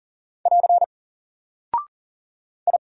Morse code, heard as a keyed tone, about 39 wpm, 700 Hz. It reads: <AR> I
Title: Single Character (Letter:Number:Punctuation:Prosign) Alt 40wpm